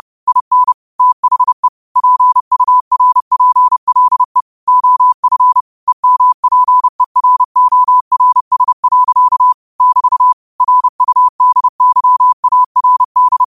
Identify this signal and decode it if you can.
Morse code, heard as a keyed tone, about 30 wpm, 1000 Hz. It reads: IN THE PURPLE OF EMPERORS. <BT> RUDYARD
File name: qotdfiles / 30wpm